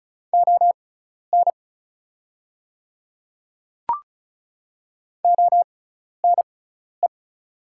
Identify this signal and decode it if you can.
Morse code, heard as a keyed tone, about 35 wpm, spaced as for 12 wpm, 690 Hz. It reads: ON ONE
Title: 6x Character Spacing - Top 100 Words - 35wpm